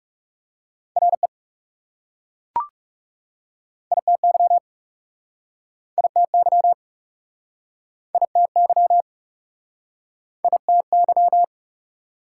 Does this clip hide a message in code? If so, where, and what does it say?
Yes: Morse code from 1.0 to 11.4 s, heard as a keyed tone, about 41 wpm, 690 Hz. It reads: RE ITY ITY ITY ITY